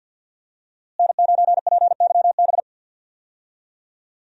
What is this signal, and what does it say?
Morse code, heard as a keyed tone, about 49 wpm, 700 Hz. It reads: N9PYB
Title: USA Call Signs - 5 Characters - Repeated 3x - 50wpm